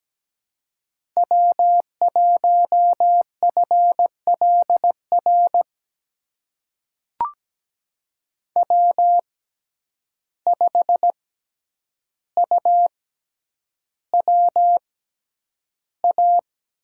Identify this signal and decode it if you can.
Morse code, heard as a keyed tone, about 17 wpm, spaced as for 9 wpm, 700 Hz. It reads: W1FLR W5UWA
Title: USA Call Signs - 5 Characters - 6x Character Spacing - 17wpm